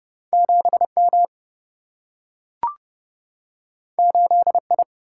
Morse code, heard as a keyed tone, about 30 wpm, 700 Hz. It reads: 7 M 8 I
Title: Single-Letter Single-Number 30wpm